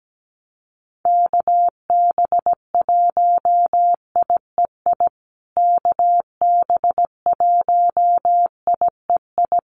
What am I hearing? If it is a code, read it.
Morse code, heard as a keyed tone, about 17 wpm, 700 Hz. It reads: KB1IEI KB1IEI
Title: USA Call Signs Repeated 2x 17wpm